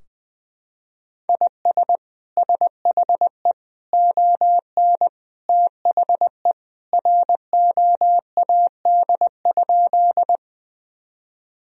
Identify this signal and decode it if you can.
Morse code, heard as a keyed tone, about 21 wpm, 700 Hz. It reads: IS SHE ON THE ROAD?
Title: Sentences from Top 1000 Words - Review Part 2 - 20wpm